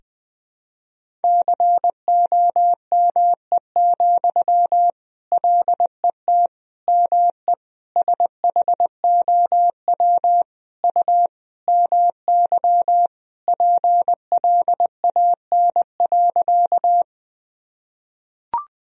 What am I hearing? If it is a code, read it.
Morse code, heard as a keyed tone, about 20 wpm, 700 Hz. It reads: COME, LET ME SHOW U MY PLAN.
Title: Sentences from Top 500 Words and Abbreviations 20wpm